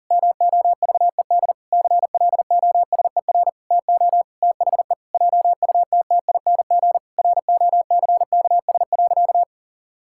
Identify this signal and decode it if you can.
Morse code, heard as a keyed tone, about 40 wpm, 700 Hz. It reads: MOVED CLOSER TO THE JUTTING ROCKS.
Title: Pangram Challenge v2 - 40wpm